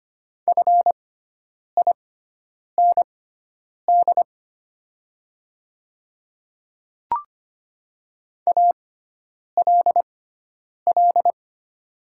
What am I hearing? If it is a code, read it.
Morse code, heard as a keyed tone, about 25 wpm, spaced as for 9 wpm, 700 Hz. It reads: FIND ALL
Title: 6x Character Spacing - Top 100 Words - 25wpm